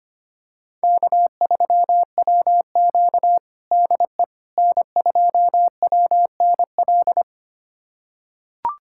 Morse code, heard as a keyed tone, about 25 wpm, 700 Hz. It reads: K3WQ DE N2WNL
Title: USA Callsign DE Callsign 25wpm